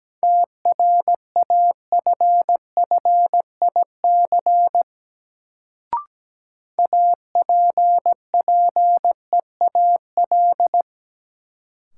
Morse code, heard as a keyed tone, about 17 wpm, 700 Hz. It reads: TRAFFIC APPEAL
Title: Sentences from Top 1300 Words - New Word Drill Alt - 17wpm